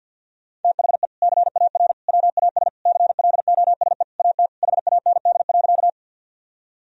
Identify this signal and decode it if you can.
Morse code, heard as a keyed tone, about 50 wpm, 700 Hz. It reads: THE CAR WAS CLOSE AT HAND.
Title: Sentences from Top 1000 Words - Review Part 1 - 50wpm